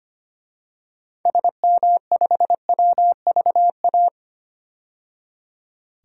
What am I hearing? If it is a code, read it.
Morse code, heard as a keyed tone, about 25 wpm, 700 Hz. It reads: SM5WVA
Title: DXCC-100 Call Signs - 6 Characters - 25wpm